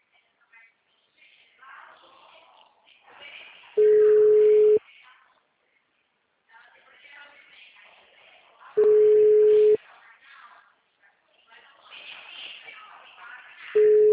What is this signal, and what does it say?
Ringback tone (Europe, 425 Hz), with speech.
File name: out- -2005- - - .10